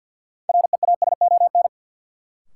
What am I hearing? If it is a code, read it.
Morse code, heard as a keyed tone, about 53 wpm, 700 Hz. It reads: REASON
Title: Sentences from Top 700 Words - New Word Drill - 50wpm